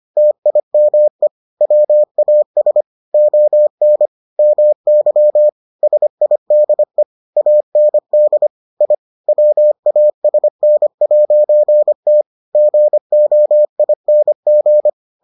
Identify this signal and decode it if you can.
Morse code, heard as a keyed tone, about 25 wpm, 600 Hz. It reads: TIME WAS ON MY SIDE AND I WASN'T GOING